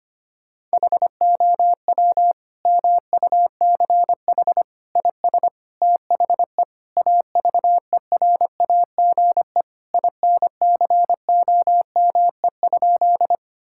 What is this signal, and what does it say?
Morse code, heard as a keyed tone, about 25 wpm, 700 Hz. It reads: HOW MUCH IS THE AVERAGE INCOME?
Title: Sentences from Top 1000 Words - Review Part 6 - 25wpm